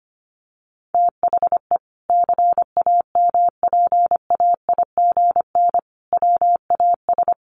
Morse code, heard as a keyed tone, about 25 wpm, 700 Hz. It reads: THE CAMPAIGN WAS
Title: Sentences from Top 1200 Words 25wpm